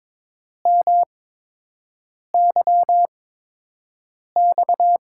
Morse code, heard as a keyed tone, about 22 wpm, spaced as for 6 wpm, 700 Hz. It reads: MYX
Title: USA Call Signs - 6 Characters - 8x Character Spacing - 22wpm